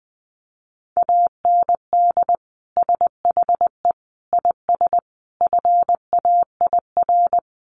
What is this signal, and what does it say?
Morse code, heard as a keyed tone, about 20 wpm, 700 Hz. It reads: AND SHE IS FAIR
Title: Sentences from Top 1300 Words 20wpm